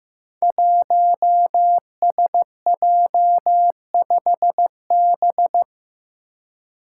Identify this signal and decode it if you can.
Morse code, heard as a keyed tone, about 15 wpm, 700 Hz. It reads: 1SJ5B
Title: Five Challenging Letter-Number Combinations 15wpm